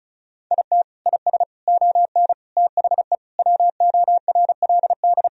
Morse code, heard as a keyed tone, about 36 wpm, 700 Hz. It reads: IT IS ON THE WORLD
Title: Sentences from Top 1000 Words - Review Part 4 - 35wpm